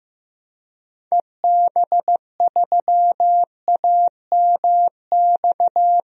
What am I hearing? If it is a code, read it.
Morse code, heard as a keyed tone, about 16 wpm, 700 Hz. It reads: EB3AMX
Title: DXCC-100 Call Signs - 6 Characters - Repeated 3x - 15wpm